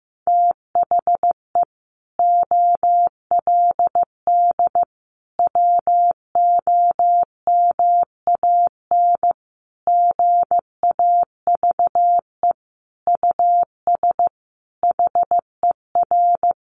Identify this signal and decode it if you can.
Morse code, heard as a keyed tone, about 15 wpm, 700 Hz. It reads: THE OLD WOMAN GAVE US HER
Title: Sentences from Top 1100 Words 15wpm